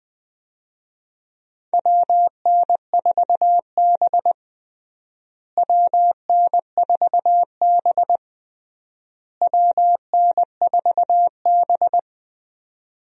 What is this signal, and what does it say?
Morse code, heard as a keyed tone, about 20 wpm, 700 Hz. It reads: WN4B WN4B WN4B
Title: USA Call Signs - 4 Characters - Repeated 3x - 20wpm